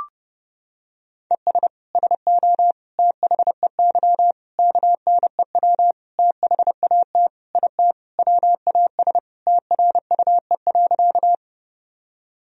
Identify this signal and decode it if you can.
Morse code, heard as a keyed tone, about 32 wpm, 700 Hz. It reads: ES SO THEY KNEW THAT IT WAS TRUE.
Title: Sentences from Top 400 Words and Abbreviations 30wpm